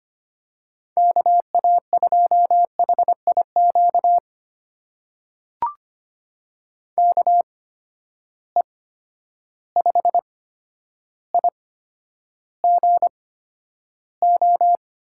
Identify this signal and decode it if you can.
Morse code, heard as a keyed tone, about 25 wpm, spaced as for 11 wpm, 700 Hz. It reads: KA2HIQ KE5IGO
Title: USA Call Signs - 6 Characters - 8x Character Spacing - 25wpm